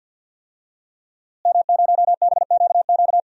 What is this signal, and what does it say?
Morse code, heard as a keyed tone, about 52 wpm, 700 Hz. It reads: M0BQQ
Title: DXCC-100 Call Signs - 5 Characters - Repeated 2x - 50wpm